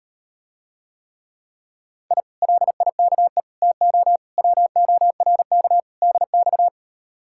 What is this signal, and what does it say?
Morse code, heard as a keyed tone, about 38 wpm, 700 Hz. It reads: I LIKE TO WORK DX